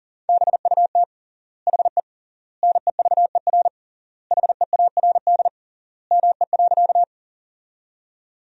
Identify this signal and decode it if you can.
Morse code, heard as a keyed tone, about 40 wpm, 700 Hz. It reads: BUT HE NEVER HEARD ME.
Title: 3x Word-Spacing - Sentences from Top 400 Words - 40wpm